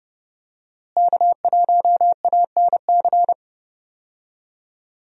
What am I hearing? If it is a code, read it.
Morse code, heard as a keyed tone, about 30 wpm, 700 Hz. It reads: K1ANC
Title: USA Call Signs - 5 Characters - Repeated 2x - 30wpm